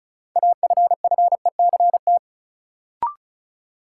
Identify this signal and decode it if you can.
Morse code, heard as a keyed tone, about 36 wpm, 700 Hz. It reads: AFFECT
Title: Sentences from Top 900 Words - New Word Drill - 35wpm